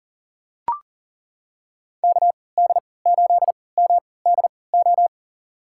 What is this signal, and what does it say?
Morse code, heard as a keyed tone, about 40 wpm, spaced as for 23 wpm, 700 Hz. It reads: KD8MDO
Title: USA Call Signs - 6 Characters - 3x Character Spacing - 40wpm